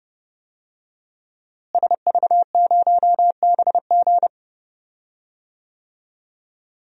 Morse code, heard as a keyed tone, about 32 wpm, 700 Hz. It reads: SV0BG
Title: DXCC-100 Call Signs - 5 Characters - Repeated 3x - 30wpm